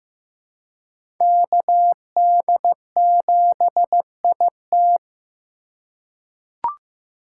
Morse code, heard as a keyed tone, about 15 wpm, 700 Hz. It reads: KD7IT
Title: USA Call Signs - 5 Characters - 4x Character Spacing - 15wpm